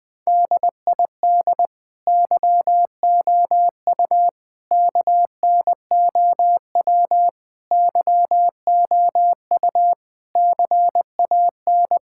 Morse code, heard as a keyed tone, about 20 wpm, 700 Hz. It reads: DID YOU KNOW YOU CAN
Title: Sets of 9 Words - 20wpm